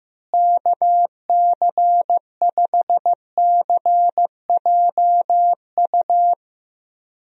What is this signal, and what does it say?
Morse code, heard as a keyed tone, about 15 wpm, 700 Hz. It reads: KC5CJU
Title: USA Call Signs - 6 Characters - Repeated 3x - 15wpm